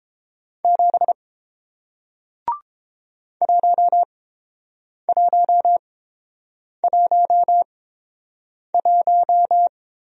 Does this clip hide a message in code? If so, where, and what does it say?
Yes: Morse code from 0.6 to 9.7 s, heard as a keyed tone, about 30 wpm, spaced as for 7 wpm, 690 Hz. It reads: 7 1111